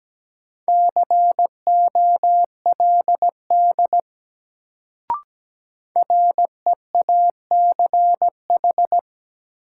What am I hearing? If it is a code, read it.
Morse code, heard as a keyed tone, about 17 wpm, 700 Hz. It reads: COLD REACH
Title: Sentences from Top 400 Words - New Word Drill - 17wpm